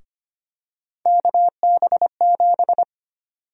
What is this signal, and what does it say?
Morse code, heard as a keyed tone, about 25 wpm, 700 Hz. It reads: KB7
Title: USA Call Sign Prefixes - Repeated 2x - 25wpm